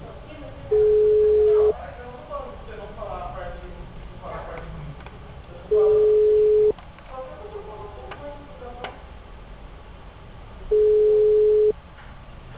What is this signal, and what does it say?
Ringback tone (Europe, 425 Hz), with speech.
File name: out- -2011- - - .13473